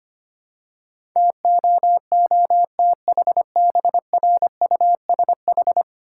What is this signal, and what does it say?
Morse code, heard as a keyed tone, about 25 wpm, 690 Hz. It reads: TOOTHBRUSH
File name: practice / 25wpm / Easy Words with Hints - 25wpm